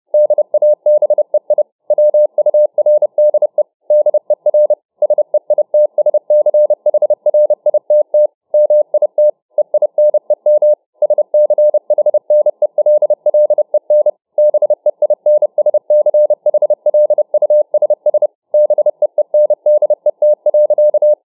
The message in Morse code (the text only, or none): DABEI WURDE DER SEITSCHRITT MIT EINEM SCHNELLEN BEINSCHLUSS BEENDET.